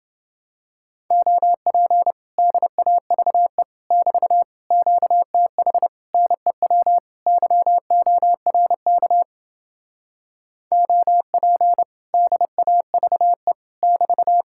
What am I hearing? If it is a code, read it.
Morse code, heard as a keyed tone, about 29 wpm, 700 Hz. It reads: OP DAVE <BT> QTH NEW YORK OP DAVE <BT>